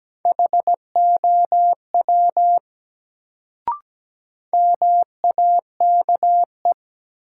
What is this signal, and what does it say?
Morse code, heard as a keyed tone, about 17 wpm, 700 Hz. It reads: HOW MAKE
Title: Sentences from Top 100 Words - New Word Drill - 17wpm